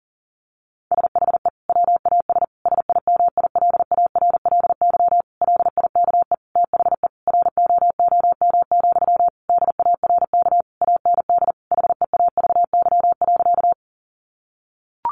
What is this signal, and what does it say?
Morse code, heard as a keyed tone, about 40 wpm, 700 Hz. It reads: SHE WAS SIMILARLY LIKE THE ROOM, DARK AND HEAVY.
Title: Sentences from Top 1400 Words 40wpm